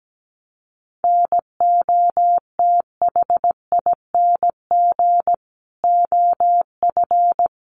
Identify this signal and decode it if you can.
Morse code, heard as a keyed tone, about 17 wpm, 700 Hz. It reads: NOTHING OF
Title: Sentences from Top 1200 Words 17wpm